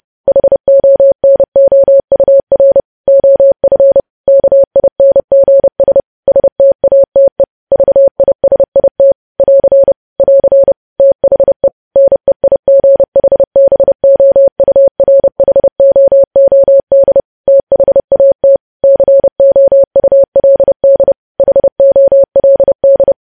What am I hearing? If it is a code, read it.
Morse code, heard as a keyed tone, about 30 wpm, 550 Hz. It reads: HONOUR OF KINGS STATE VISIT <AR> <AR> THE NEIGHBOURHOOD THAT COULD HOLD
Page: news headlines in morse code at 30 wpm